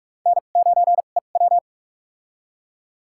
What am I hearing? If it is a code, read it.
Morse code, heard as a keyed tone, about 42 wpm, spaced as for 31 wpm, 700 Hz. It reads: N9EW
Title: USA Call Signs - 4 Characters - 2x Character Spacing - 45wpm